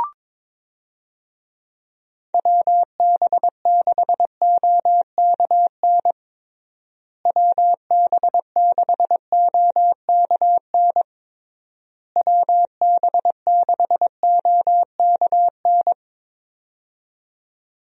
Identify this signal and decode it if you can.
Morse code, heard as a keyed tone, about 22 wpm, 700 Hz. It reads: WB6OKN WB6OKN WB6OKN
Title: USA Call Signs - 6 Characters - Repeated 3x - 22wpm